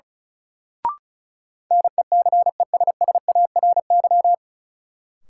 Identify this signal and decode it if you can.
Morse code, heard as a keyed tone, about 34 wpm, 700 Hz. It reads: NECESSARY
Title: Sentences from Top 800 Words - New Word Drill - 35wpm